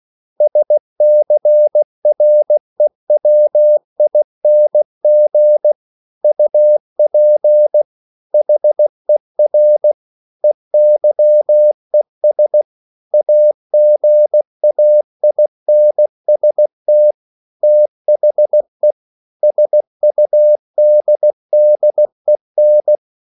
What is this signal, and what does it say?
Morse code, heard as a keyed tone, about 16 wpm, 600 Hz. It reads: SCREWING UP HER EYES AGAINST THE SUDDEN